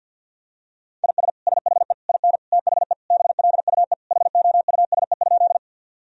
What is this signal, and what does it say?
Morse code, heard as a keyed tone, about 46 wpm, 700 Hz. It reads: IS SHE IN THE BLUE HOUSE?
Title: Sentences from Top 1000 Words - Review Part 3 - 50wpm